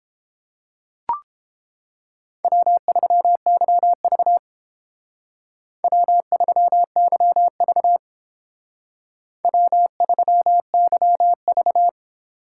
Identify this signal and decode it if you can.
Morse code, heard as a keyed tone, about 30 wpm, 700 Hz. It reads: W3YV W3YV W3YV